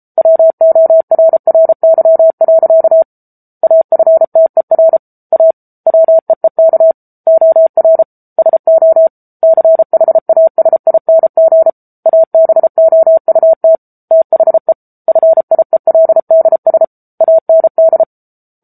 Morse code, heard as a keyed tone, about 34 wpm, 650 Hz. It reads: WORRY. AFTER A WEEK OR SO CHASING ABOUT THE FIELDS AND
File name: Book Two / 07-Chapter 7 - 33 WPM